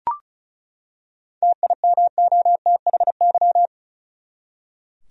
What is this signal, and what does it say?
Morse code, heard as a keyed tone, about 34 wpm, 700 Hz. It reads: TIMOTHY
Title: USA Common Names - 35wpm